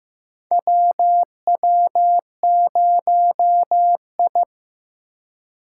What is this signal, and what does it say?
Morse code, heard as a keyed tone, about 15 wpm, 700 Hz. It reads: WW0I
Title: USA Call Signs - 4 Characters - 3x Character Spacing - 15wpm